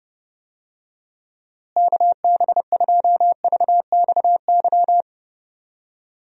Morse code, heard as a keyed tone, about 30 wpm, 700 Hz. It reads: KB2VXY
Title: USA Call Signs - 6 Characters - Repeated 3x - 30wpm